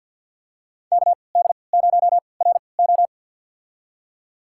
Morse code, heard as a keyed tone, about 52 wpm, spaced as for 29 wpm, 700 Hz. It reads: KD0RO
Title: USA Call Signs - 5 Characters - 3x Character Spacing - 50wpm